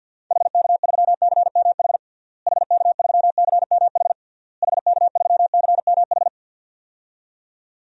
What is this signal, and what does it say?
Morse code, heard as a keyed tone, about 52 wpm, 700 Hz. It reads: HK3CMH HK3CMH HK3CMH
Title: DXCC-100 Call Signs - 6 Characters - Repeated 3x - 50wpm